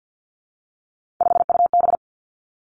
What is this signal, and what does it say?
Morse code, heard as a keyed tone, about 50 wpm, 700 Hz. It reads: 5UB
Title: Three Challenging Letter-Number Combinations 50wpm